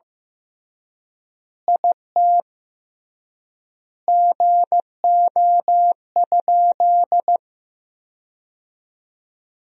Morse code, heard as a keyed tone, about 15 wpm, 700 Hz. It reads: IT GO?
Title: 3x Word-Spacing - Sentences from Top 200 Words - 15wpm